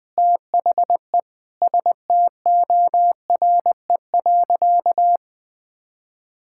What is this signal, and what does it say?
Morse code, heard as a keyed tone, about 20 wpm, 700 Hz. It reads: THE STORE.
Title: Sentences with 4-Word Phrases - 20wpm